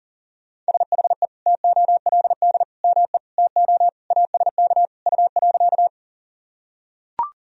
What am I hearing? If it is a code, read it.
Morse code, heard as a keyed tone, about 37 wpm, 700 Hz. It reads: SHE TOLD ME TO ASK U.
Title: Sentences from Top 400 Words and Abbreviations 40wpm